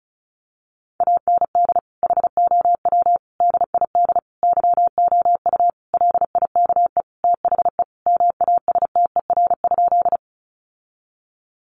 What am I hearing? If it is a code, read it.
Morse code, heard as a keyed tone, about 34 wpm, 700 Hz. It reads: AND HOW DID YOU LIKE THE MASTER?
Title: Sentences from Top 1300 Words 35wpm